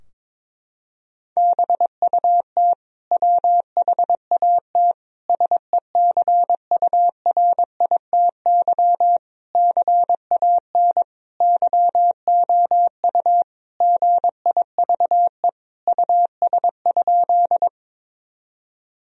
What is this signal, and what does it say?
Morse code, heard as a keyed tone, about 22 wpm, 700 Hz. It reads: BUT WHAT SECURITY CAN YOU GIVE US?
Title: Sentences from Top 1000 Words - Review Part 4 - 22wpm